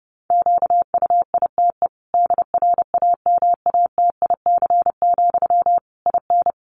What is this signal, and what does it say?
Morse code, heard as a keyed tone, about 28 wpm, 700 Hz. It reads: QUITE DRAMATIC, IN
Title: Sentences from Top 1300 Words 30wpm